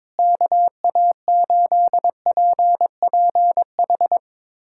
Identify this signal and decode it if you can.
Morse code, heard as a keyed tone, about 22 wpm, 700 Hz. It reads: KA8PPH
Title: USA Callsign DE Callsign - 6 Characters - 22wpm